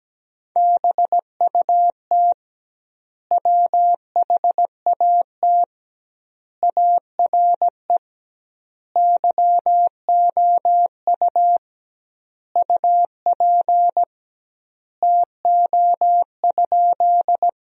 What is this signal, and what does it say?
Morse code, heard as a keyed tone, about 17 wpm, 700 Hz. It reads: BUT WHAT ARE YOU UP TO?
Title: 2x Word-Spacing - Sentences from Top 100 Words - 17wpm